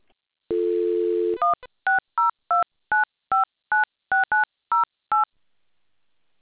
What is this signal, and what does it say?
DTMF, heard as telephone touch tones, tones of about 150 ms. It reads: 16029596908